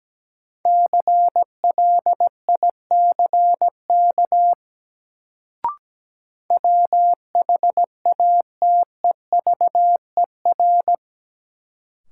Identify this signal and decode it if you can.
Morse code, heard as a keyed tone, about 17 wpm, 700 Hz. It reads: CLICK WHATEVER
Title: Sentences from Top 1100 Words - New Word Drill Alt - 17wpm